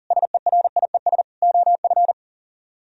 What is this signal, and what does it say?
Morse code, heard as a keyed tone, about 51 wpm, 700 Hz. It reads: SERIES OF